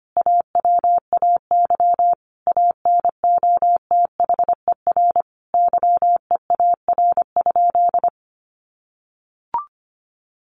Morse code, heard as a keyed tone, about 25 wpm, 700 Hz. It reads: AWAY ANOTHER YEAR?
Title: Sentences from Top 1500 Words 25wpm